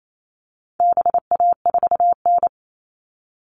Morse code, heard as a keyed tone, about 28 wpm, 700 Hz. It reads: BA4N
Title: All Cut Numbers - 4-Digit 28wpm